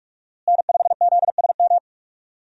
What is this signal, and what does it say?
Morse code, heard as a keyed tone, about 46 wpm, 700 Hz. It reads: N5ZSM
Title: USA Call Signs - 5 Characters - Repeated 3x - 45wpm